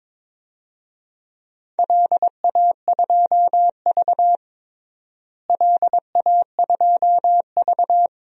Morse code, heard as a keyed tone, about 22 wpm, 700 Hz. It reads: LA2V LA2V
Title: DXCC-100 Call Signs - 4 Characters - Repeated 3x - 22wpm